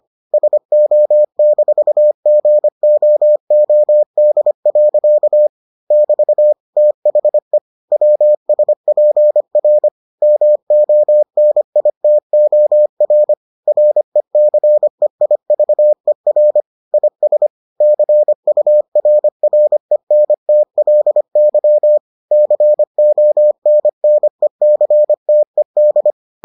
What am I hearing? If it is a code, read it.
Morse code, heard as a keyed tone, about 25 wpm, 600 Hz. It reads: SO-GOOD. <BT> THE WSPR MONITOR RECEIVER IS CURRENTLY CONNECTED